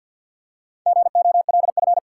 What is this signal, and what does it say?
Morse code, heard as a keyed tone, about 48 wpm, 700 Hz. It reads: GOLF
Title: Four Letter Words - 50wpm